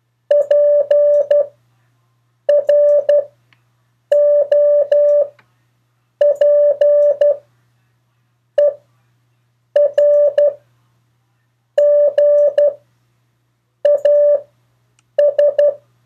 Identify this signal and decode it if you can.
Morse code, heard as a keyed tone, about 12 wpm, spaced as for 6 wpm, 580 Hz. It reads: PROPERGAS
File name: 4of4-sentences-mistakes-test